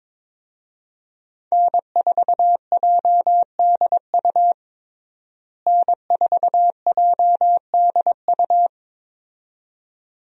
Morse code, heard as a keyed tone, about 22 wpm, 700 Hz. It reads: N4JDU N4JDU
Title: USA Call Signs - 5 Characters - Repeated 2x - 22wpm